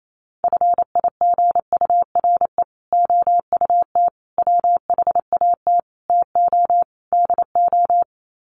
Morse code, heard as a keyed tone, about 28 wpm, 700 Hz. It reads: FIGURE OUT WHAT TO DO
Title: Sets of 5 Words Encore2 28wpm